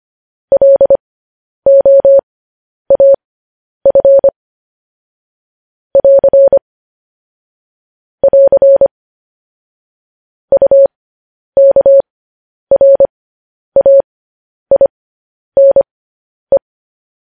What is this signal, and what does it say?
Morse code, heard as a keyed tone, about 25 wpm, spaced as for 10 wpm, 550 Hz. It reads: LOAF <AR> <AR> UKRAINE
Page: news headlines in morse code at 10 wpm